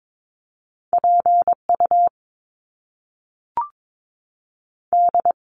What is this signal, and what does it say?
Morse code, heard as a keyed tone, about 22 wpm, 700 Hz. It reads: PU D
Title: Top Two Letter Combinations in Words 22wpm